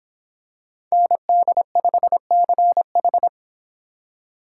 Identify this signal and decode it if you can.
Morse code, heard as a keyed tone, about 26 wpm, 700 Hz. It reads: ND5CH